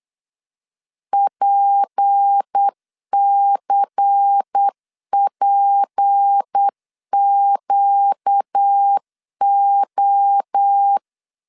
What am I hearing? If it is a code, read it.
Morse code, heard as a keyed tone, about 8 wpm, 800 Hz. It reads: PCPQO
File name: LEZIONI / 6 PKX0 / 6 P 8wpm 3025